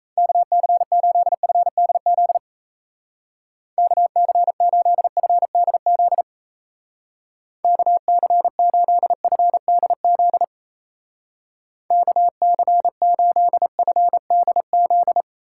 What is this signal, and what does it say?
Morse code, heard as a keyed tone, about 36 wpm, 700 Hz. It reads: KC8FDZ KC8FDZ KC8FDZ KC8FDZ